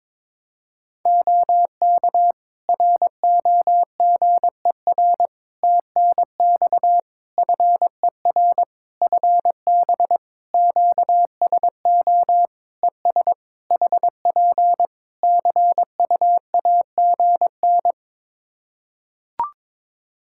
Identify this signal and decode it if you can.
Morse code, heard as a keyed tone, about 22 wpm, 700 Hz. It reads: OK ROGER TNX FER FB QSO ES HP CUAGN
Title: Practice QSO - Exchange 4 - Ending - 22wpm